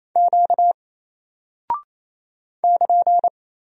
Morse code, heard as a keyed tone, about 28 wpm, 700 Hz. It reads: Q <KN>
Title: Single Character (Letter:Number:Punctuation:Prosign) 28wpm